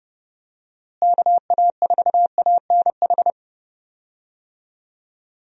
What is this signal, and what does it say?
Morse code, heard as a keyed tone, about 30 wpm, 700 Hz. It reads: KA4ANH
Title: USA Call Signs - 6 Characters - Repeated 2x - 30wpm